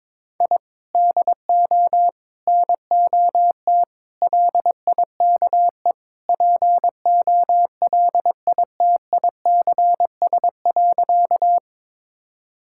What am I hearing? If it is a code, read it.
Morse code, heard as a keyed tone, about 21 wpm, 700 Hz. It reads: I DO NOT LIKE POLITICS.
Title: Sentences from Top 1000 Words - Review Part 6 - 22wpm